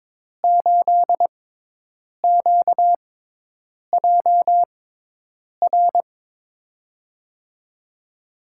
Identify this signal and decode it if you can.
Morse code, heard as a keyed tone, about 22 wpm, spaced as for 8 wpm, 700 Hz. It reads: 8QJR
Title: USA Call Signs - 5 Characters - 6x Character Spacing - 22wpm